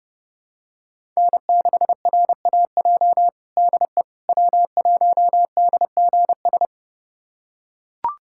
Morse code, heard as a keyed tone, about 31 wpm, 700 Hz. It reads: N6RAJ DE W1DGS
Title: USA Callsign DE Callsign - 5 Characters - 30wpm